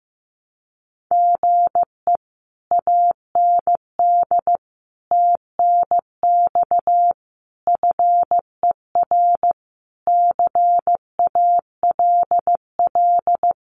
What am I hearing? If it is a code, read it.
Morse code, heard as a keyed tone, about 15 wpm, 700 Hz. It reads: GE AND TNX FER CALL
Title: Common QSO Phrases Encore 15wpm